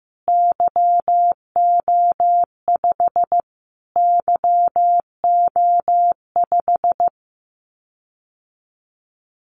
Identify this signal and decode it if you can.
Morse code, heard as a keyed tone, about 15 wpm, 700 Hz. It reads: YO5 YO5
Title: Callsign Prefixes for Top 25 DXCC - Repeated 3x - 15wpm